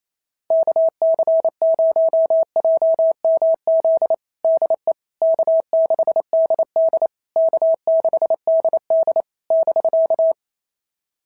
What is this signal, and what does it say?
Morse code, heard as a keyed tone, about 28 wpm, 650 Hz. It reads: KC0JMZ DE K6DD K6DD <BK>
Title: SOTA QSO Practice 28wpm